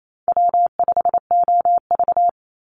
Morse code, heard as a keyed tone, about 28 wpm, 700 Hz. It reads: W5OV
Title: Call Signs - 28wpm